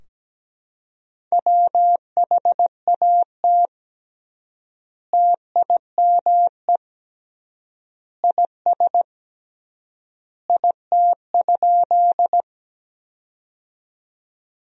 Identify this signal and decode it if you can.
Morse code, heard as a keyed tone, about 17 wpm, 700 Hz. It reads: WHAT TIME IS IT?
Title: 3x Word-Spacing - Sentences from Top 100 Words - 17wpm